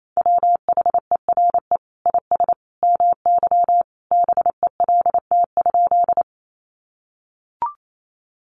Morse code, heard as a keyed tone, about 28 wpm, 700 Hz. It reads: WHERE IS MY BELT?
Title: Sentences from Top 1400 Words 28wpm